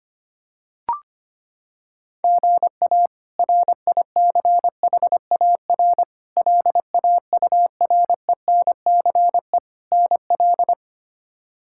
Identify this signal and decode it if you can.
Morse code, heard as a keyed tone, about 25 wpm, 700 Hz. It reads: GA RICHAR LAURENCE NL
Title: K1USN SST QSO Practice - 25wpm